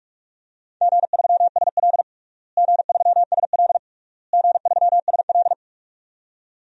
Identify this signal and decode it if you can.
Morse code, heard as a keyed tone, about 46 wpm, 700 Hz. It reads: G3SL G3SL G3SL
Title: DXCC-100 Call Signs - 4 Characters - Repeated 3x - 45wpm